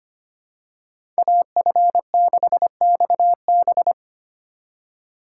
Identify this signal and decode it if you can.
Morse code, heard as a keyed tone, about 25 wpm, 700 Hz. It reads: AF6XB
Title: USA Call Signs - 5 Characters - 3x Character Spacing - 25wpm